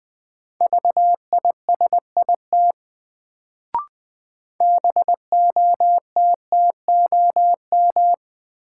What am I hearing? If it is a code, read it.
Morse code, heard as a keyed tone, about 20 wpm, 700 Hz. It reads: VISIT BOTTOM
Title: Sentences from Top 900 Words - New Word Drill - 20wpm